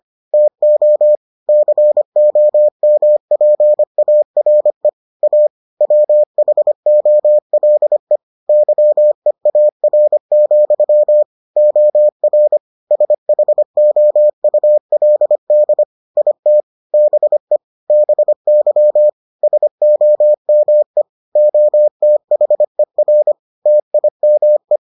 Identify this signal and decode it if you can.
Morse code, heard as a keyed tone, about 25 wpm, 600 Hz. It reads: TO COMPARE A WHOLE YEAR, OR SHOULD IT BE BY SOME OTHER TIME